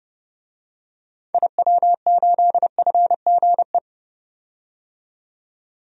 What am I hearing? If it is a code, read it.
Morse code, heard as a keyed tone, about 31 wpm, 700 Hz. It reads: IW8FGE
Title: DXCC-100 Call Signs - 6 Characters - 30wpm